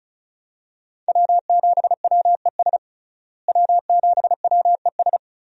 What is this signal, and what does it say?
Morse code, heard as a keyed tone, about 36 wpm, 700 Hz. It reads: W7WES W7WES
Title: USA Call Signs - 5 Characters - Repeated 2x - 35wpm